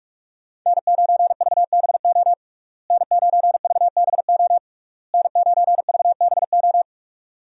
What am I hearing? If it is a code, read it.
Morse code, heard as a keyed tone, about 46 wpm, 700 Hz. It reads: N9VBO N9VBO N9VBO
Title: USA Call Signs - 5 Characters - Repeated 3x - 45wpm